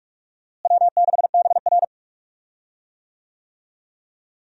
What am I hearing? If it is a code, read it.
Morse code, heard as a keyed tone, about 46 wpm, 700 Hz. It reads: W6BR
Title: USA Call Signs - 4 Characters - Repeated 2x - 45wpm